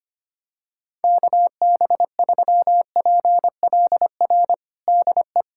Morse code, heard as a keyed tone, about 25 wpm, 700 Hz. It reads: KB3PLR DE
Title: USA Callsign DE Callsign 25wpm